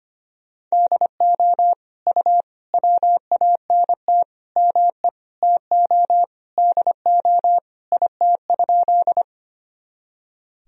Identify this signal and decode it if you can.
Morse code, heard as a keyed tone, about 25 wpm, 700 Hz. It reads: DO U WANT ME TO DO IT?
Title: Sentences from Top 200 Words and Abbreviations 25wpm